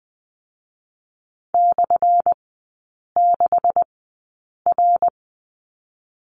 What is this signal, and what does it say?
Morse code, heard as a keyed tone, about 20 wpm, spaced as for 8 wpm, 700 Hz. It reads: /6R
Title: 184 Lesson 30e - Letter b - Three Characters 20wpm